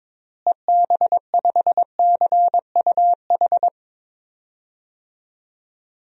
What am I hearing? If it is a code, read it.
Morse code, heard as a keyed tone, about 21 wpm, 700 Hz. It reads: EB5CUH
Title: DXCC-100 Call Signs - 6 Characters - Repeated 3x - 22wpm